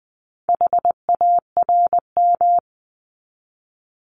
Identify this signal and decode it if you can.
Morse code, heard as a keyed tone, about 20 wpm, 700 Hz. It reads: HARM
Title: 260 Lesson 42e - Words 20-20wpm